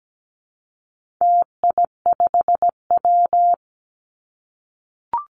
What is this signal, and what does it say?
Morse code, heard as a keyed tone, about 17 wpm, 700 Hz. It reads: TI5W
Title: Call Signs - 17wpm